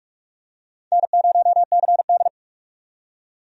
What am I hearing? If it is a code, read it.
Morse code, heard as a keyed tone, about 46 wpm, 700 Hz. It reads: N0CD
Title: USA Call Signs - 4 Characters - 45wpm